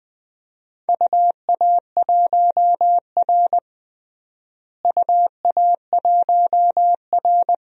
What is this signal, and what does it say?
Morse code, heard as a keyed tone, about 20 wpm, 700 Hz. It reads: UA1R UA1R
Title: DXCC-100 Call Signs - 4 Characters - Repeated 3x - 20wpm